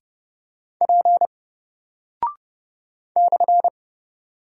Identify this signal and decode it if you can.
Morse code, heard as a keyed tone, about 30 wpm, 700 Hz. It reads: P /
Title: Single Character (Letter:Number:Punctuation:Prosign) 30wpm